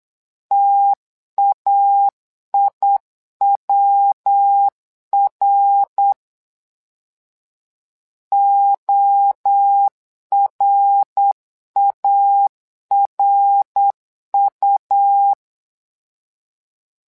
Morse code, heard as a keyed tone, about 8 wpm, 800 Hz. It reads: TAIWR ORARU
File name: LEZIONI / 3 MWRO / 3 MWRO 8wpm 2606